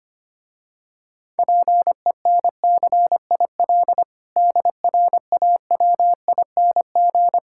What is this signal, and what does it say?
Morse code, heard as a keyed tone, about 25 wpm, 690 Hz. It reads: PENCIL DRAWING
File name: practice / 25wpm / Difficult Words - Speed-Racing with Hints - 25wpm